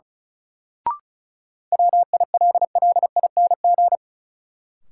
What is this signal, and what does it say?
Morse code, heard as a keyed tone, about 34 wpm, 700 Hz. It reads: WILLING
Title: Sentences from Top 1000 Words - New Word Drill - 35wpm